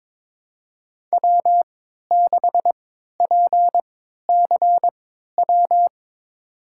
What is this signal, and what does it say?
Morse code, heard as a keyed tone, about 22 wpm, spaced as for 12 wpm, 700 Hz. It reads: W6PCW
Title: USA Call Signs - 5 Characters - 3x Character Spacing - 22wpm